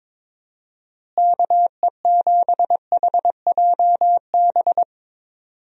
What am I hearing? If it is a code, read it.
Morse code, heard as a keyed tone, about 22 wpm, 700 Hz. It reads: KE7HJB
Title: USA Call Signs - 6 Characters - Repeated 2x - 22wpm